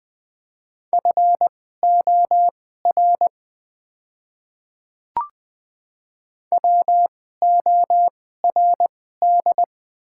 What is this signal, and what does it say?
Morse code, heard as a keyed tone, about 20 wpm, spaced as for 14 wpm, 700 Hz. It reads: FOR WORD
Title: 2x Character Spacing - Top 100 Words - 20wpm